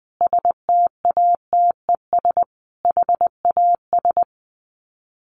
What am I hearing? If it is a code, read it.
Morse code, heard as a keyed tone, about 20 wpm, 700 Hz. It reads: STATES HAS
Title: Sets of 5 Words Encore2 20wpm